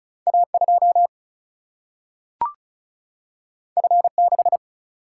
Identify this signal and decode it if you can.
Morse code, heard as a keyed tone, about 36 wpm, 700 Hz. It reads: A2 F6
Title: Single-Letter Single-Number 35wpm